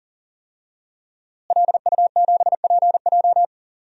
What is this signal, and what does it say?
Morse code, heard as a keyed tone, about 40 wpm, 700 Hz. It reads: LU7PJ
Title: DXCC-100 Call Signs - 5 Characters - Repeated 3x - 40wpm